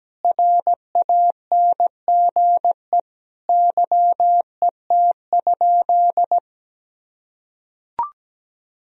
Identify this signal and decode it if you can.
Morse code, heard as a keyed tone, about 17 wpm, 700 Hz. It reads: RANGE YET?
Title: Sentences from Top 1000 Words - Review Part 4 - 17wpm